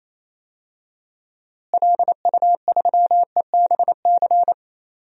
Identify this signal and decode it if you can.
Morse code, heard as a keyed tone, about 28 wpm, 700 Hz. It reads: LU3EBC
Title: DXCC-100 Call Signs - 6 Characters - Repeated 3x - 28wpm